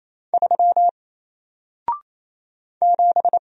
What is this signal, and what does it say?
Morse code, heard as a keyed tone, about 27 wpm, 700 Hz. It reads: 3 7
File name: Single-Letter-Number-28wpm